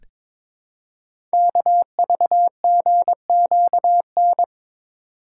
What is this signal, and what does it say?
Morse code, heard as a keyed tone, about 22 wpm, 700 Hz. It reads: KVGQN
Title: Code Groups - 5 Letters - 22wpm